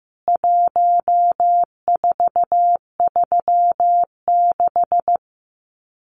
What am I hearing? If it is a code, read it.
Morse code, heard as a keyed tone, about 15 wpm, 700 Hz. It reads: 1436
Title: CWT Exchange 15wpm